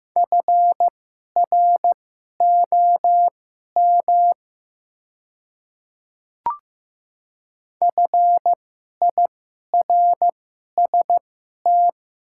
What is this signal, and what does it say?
Morse code, heard as a keyed tone, about 15 wpm, spaced as for 11 wpm, 700 Hz. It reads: FROM FIRST
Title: 2x Character Spacing - Top 100 Words - 15wpm